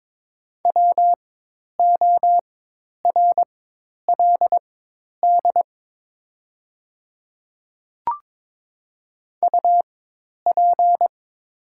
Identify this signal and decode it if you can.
Morse code, heard as a keyed tone, about 22 wpm, spaced as for 10 wpm, 700 Hz. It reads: WORLD UP
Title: 4x Character Spacing - Top 200 Words - 22wpm